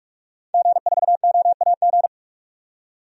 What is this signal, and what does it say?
Morse code, heard as a keyed tone, about 46 wpm, 700 Hz. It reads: G4OAG
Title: DXCC-100 Call Signs - 5 Characters - 45wpm